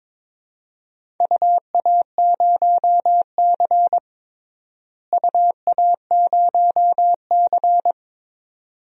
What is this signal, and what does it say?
Morse code, heard as a keyed tone, about 22 wpm, 700 Hz. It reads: UA0C UA0C
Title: DXCC-100 Call Signs - 4 Characters - Repeated 2x - 22wpm